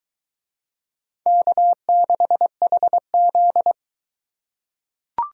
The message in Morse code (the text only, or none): K6HZ